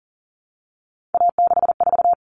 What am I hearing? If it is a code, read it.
Morse code, heard as a keyed tone, about 40 wpm, 700 Hz. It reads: A64
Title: Common Cut Numbers - 3-Digit 40wpm